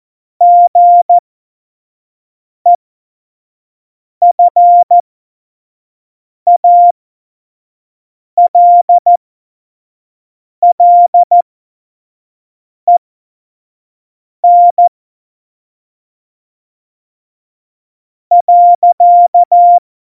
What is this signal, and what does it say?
Morse code, heard as a keyed tone, about 14 wpm, spaced as for 4 wpm, 700 Hz. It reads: GEFALLEN.
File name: Morsekurs / Witze / Witz 12. Dezember 2017, 25 F70